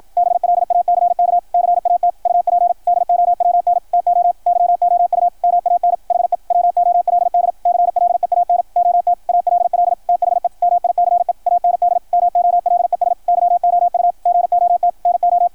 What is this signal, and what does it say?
Morse code, heard as a keyed tone, about 53 wpm, 690 Hz, with radio noise. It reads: BLACK CAT UP DOWN TO YOU MAN HE WOLD CLEAN OT ALL THE MICE AND MOLES YOU GOT NO
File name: archived / CRSnet